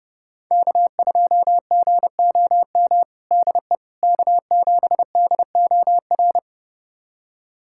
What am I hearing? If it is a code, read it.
Morse code, heard as a keyed tone, about 29 wpm, 700 Hz. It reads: K2GOM DE K7DOR
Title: USA Callsign DE Callsign - 5 Characters - 30wpm